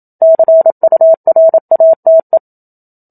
Morse code, heard as a keyed tone, about 27 wpm, 650 Hz. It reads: CURATE
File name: Book Two / 01-Chapter 1 - 27 WPM